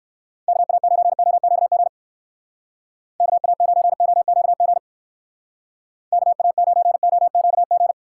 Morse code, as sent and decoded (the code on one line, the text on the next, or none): -.- .- ----. --- --.- --. / -.- .- ----. --- --.- --. / -.- .- ----. --- --.- --.
KA9OQG KA9OQG KA9OQG